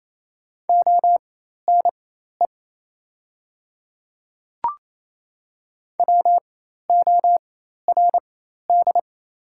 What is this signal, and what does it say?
Morse code, heard as a keyed tone, about 27 wpm, spaced as for 13 wpm, 700 Hz. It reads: ONE WORD
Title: 4x Character Spacing - Top 300 Words - 28wpm